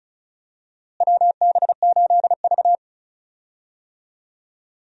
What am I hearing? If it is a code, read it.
Morse code, heard as a keyed tone, about 36 wpm, 700 Hz. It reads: WB8V
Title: USA Call Signs - 4 Characters - Repeated 3x - 35wpm